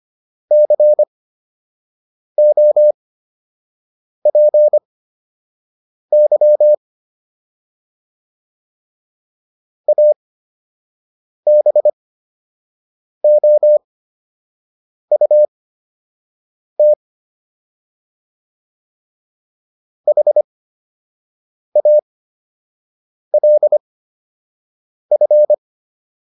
Morse code, heard as a keyed tone, about 25 wpm, spaced as for 6 wpm, 600 Hz. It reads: COPY ABOUT HALF